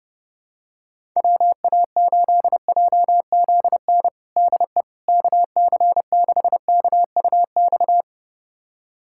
Morse code, heard as a keyed tone, about 30 wpm, 700 Hz. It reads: WA8JZN DE KC6KUX
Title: USA Callsign DE Callsign - 6 Characters - 30wpm